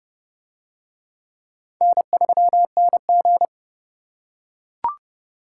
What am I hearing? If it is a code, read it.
Morse code, heard as a keyed tone, about 30 wpm, 700 Hz. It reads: N3NG